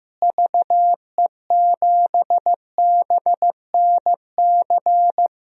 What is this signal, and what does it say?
Morse code, heard as a keyed tone, about 16 wpm, 700 Hz. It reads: VE7BNC
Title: DXCC-100 Call Signs - 6 Characters - Repeated 2x - 15wpm